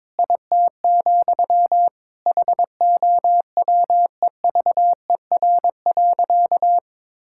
Morse code, heard as a keyed tone, about 22 wpm, 700 Hz. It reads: IT, HOWEVER.
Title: Sentences from Top 1000 Words - Review Part 6 - 22wpm